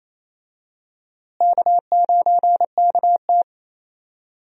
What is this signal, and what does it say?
Morse code, heard as a keyed tone, about 28 wpm, 700 Hz. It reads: K9KT
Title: USA Call Signs - 4 Characters - Repeated 2x - 28wpm